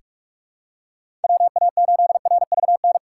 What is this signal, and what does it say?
Morse code, heard as a keyed tone, about 45 wpm, 700 Hz. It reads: WA8RVN
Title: USA Call Signs - 6 Characters - 3x Character Spacing - 45wpm